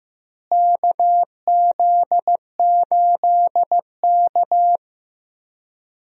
Morse code, heard as a keyed tone, about 16 wpm, 700 Hz. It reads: KZ8K
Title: USA Call Signs - 4 Characters - Repeated 3x - 15wpm